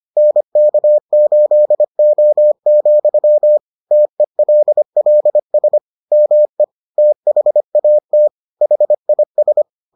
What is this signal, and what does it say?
Morse code, heard as a keyed tone, about 25 wpm, 600 Hz. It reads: NK8O, TELLS ME THAT HIS